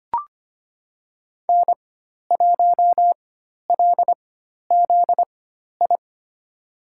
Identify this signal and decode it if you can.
Morse code, heard as a keyed tone, about 25 wpm, spaced as for 12 wpm, 700 Hz. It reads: N1LZI
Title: USA Call Signs - 4x Character Spacing - 25wpm